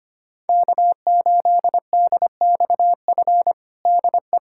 Morse code, heard as a keyed tone, about 25 wpm, 700 Hz. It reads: K8DXF DE
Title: USA Callsign DE Callsign - 5 Characters - 25wpm